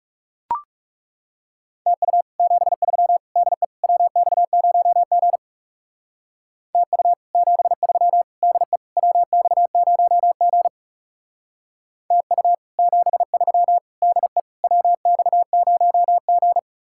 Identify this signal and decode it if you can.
Morse code, heard as a keyed tone, about 37 wpm, 700 Hz. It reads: TU 73 DE WX0G TU 73 DE WX0G TU 73 DE WX0G
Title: Speed-Racing Common QSO Phrases - 30wpm